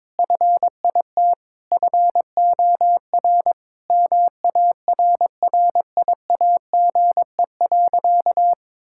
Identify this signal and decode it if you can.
Morse code, heard as a keyed tone, about 22 wpm, 700 Hz. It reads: FIT FOR MARRIAGE.
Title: Sentences from Top 1000 Words - Review Part 5 - 22wpm